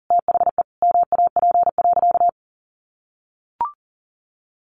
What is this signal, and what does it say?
Morse code, heard as a keyed tone, about 40 wpm, 700 Hz. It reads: THE MAP.
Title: Sentences from Top 400 Words 40wpm